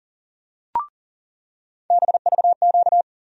Morse code, heard as a keyed tone, about 40 wpm, 700 Hz. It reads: BVQ
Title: USA Call Sign Postfixes - 3 Letters 40wpm